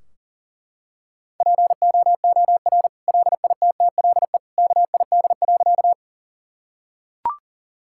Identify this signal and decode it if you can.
Morse code, heard as a keyed tone, about 40 wpm, 700 Hz. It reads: POOR LITTLE KID.
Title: Sentences from Top 1600 Words 40wpm